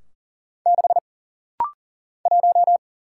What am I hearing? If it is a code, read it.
Morse code, heard as a keyed tone, about 40 wpm, 700 Hz. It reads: 6 1
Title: Single Letter-Number Alt - Rapid-Fire 40wpm